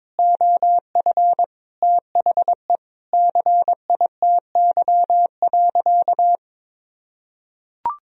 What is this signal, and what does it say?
Morse code, heard as a keyed tone, about 22 wpm, 700 Hz. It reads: OF THE CITY.
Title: Sentences from Top 1000 Words - Review Part 1 - 22wpm